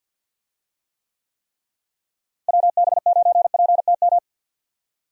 Morse code, heard as a keyed tone, about 50 wpm, 700 Hz. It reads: WB9PTM
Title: USA Call Signs - 6 Characters - Repeated 3x - 50wpm